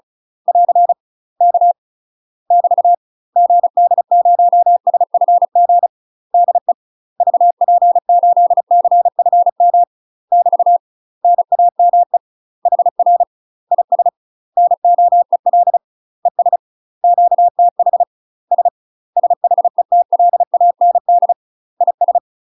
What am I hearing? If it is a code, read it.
Morse code, heard as a keyed tone, about 34 wpm, 700 Hz. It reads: <AR> K <BT> GD0SFG DE VP8CFM <BT> NAME HR IS NOEL ES QTH S SHETLAND IS
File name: UK MASTER QSOs TRAINING/35WPM